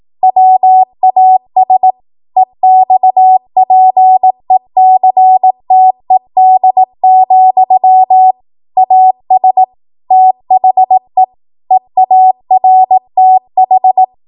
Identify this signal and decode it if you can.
Morse code, heard as a keyed tone, about 18 wpm, 750 Hz. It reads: WAS EXPECTED, AS THE EARTH
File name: Morse / 18WPM